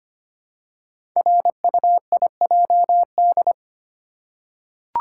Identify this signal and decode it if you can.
Morse code, heard as a keyed tone, about 25 wpm, 700 Hz. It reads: RUIJD
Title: Code Groups - 5 Letters - 25wpm